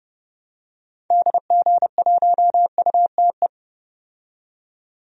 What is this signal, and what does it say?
Morse code, heard as a keyed tone, about 31 wpm, 700 Hz. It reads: DG1UTE
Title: DXCC-100 Call Signs - 6 Characters - 30wpm